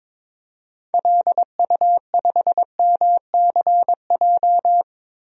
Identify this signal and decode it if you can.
Morse code, heard as a keyed tone, about 22 wpm, 700 Hz. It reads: LU5MCJ
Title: DXCC-100 Call Signs - 6 Characters - 22wpm